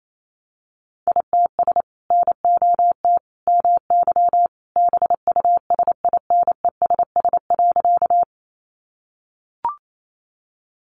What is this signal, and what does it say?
Morse code, heard as a keyed tone, about 28 wpm, 700 Hz. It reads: ITS NOT MY BUSINESS.
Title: Sentences from Top 600 Words 28wpm